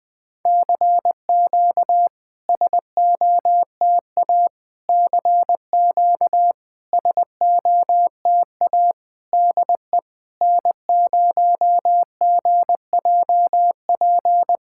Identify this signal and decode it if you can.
Morse code, heard as a keyed tone, about 20 wpm, 700 Hz. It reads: CQ SOTA CQ SOTA DE N0GJP
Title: SOTA QSO Practice 20wpm